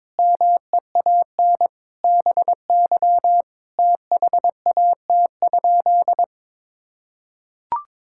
Morse code, heard as a keyed tone, about 22 wpm, 700 Hz. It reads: MEAN BY THAT?
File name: practice / 22wpm / Sentences with 4-Word Phrases - 22wpm